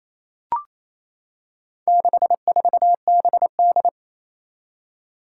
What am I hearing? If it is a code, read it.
Morse code, heard as a keyed tone, about 28 wpm, 700 Hz. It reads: 64BD
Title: Four Challenging Letter-Number Combinations 28wpm